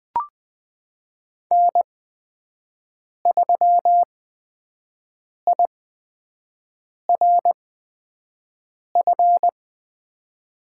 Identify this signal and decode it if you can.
Morse code, heard as a keyed tone, about 20 wpm, spaced as for 5 wpm, 700 Hz. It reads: N3IRF
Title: USA Call Signs - 8x Character Spacing - 20wpm